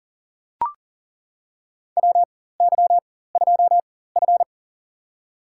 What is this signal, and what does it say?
Morse code, heard as a keyed tone, about 40 wpm, spaced as for 19 wpm, 700 Hz. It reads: WY2F
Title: USA Call Signs - 4x Character Spacing - 40wpm